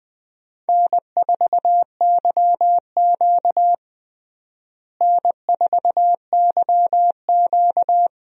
Morse code, heard as a keyed tone, about 21 wpm, 700 Hz. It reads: N4YQ N4YQ
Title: USA Call Signs - 4 Characters - Repeated 2x - 20wpm